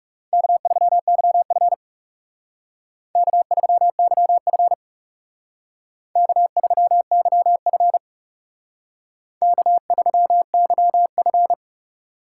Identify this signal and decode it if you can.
Morse code, heard as a keyed tone, about 38 wpm, 700 Hz. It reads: K3YF K3YF K3YF K3YF